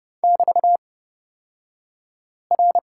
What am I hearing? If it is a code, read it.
Morse code, heard as a keyed tone, about 31 wpm, spaced as for 5 wpm, 700 Hz. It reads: <BT>R
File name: Lesson 38d - Letter q - Two Characters 30wpm